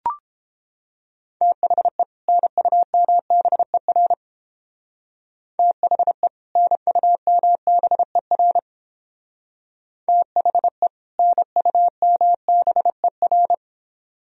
Morse code, heard as a keyed tone, about 30 wpm, 700 Hz. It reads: THE NUMBER THE NUMBER THE NUMBER